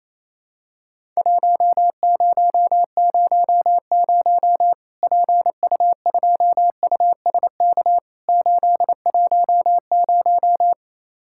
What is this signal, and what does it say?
Morse code, heard as a keyed tone, about 28 wpm, 700 Hz. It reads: 1000 PU2USK 810
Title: Worked All Europe DX Contest - QTC Exchange - 28wpm